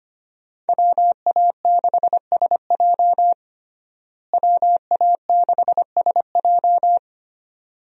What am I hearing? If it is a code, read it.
Morse code, heard as a keyed tone, about 25 wpm, 700 Hz. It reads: WA6SJ WA6SJ
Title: USA Call Signs - 5 Characters - Repeated 2x - 25wpm